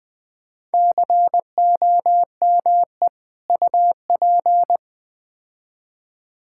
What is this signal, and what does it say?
Morse code, heard as a keyed tone, about 20 wpm, 700 Hz. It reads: COME UP
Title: Build A Phrase - 4-Word Phrases - 20wpm